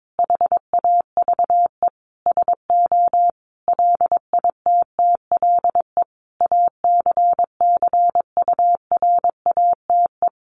Morse code, heard as a keyed tone, about 22 wpm, 700 Hz. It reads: HAVE SO LITTLE ACCURATE
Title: Sentences from Top 1200 Words 22wpm